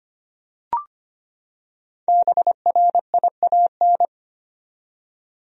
Morse code, heard as a keyed tone, about 25 wpm, 700 Hz. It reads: BRIAN
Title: USA Common Names - Max 5 Letters - 25wpm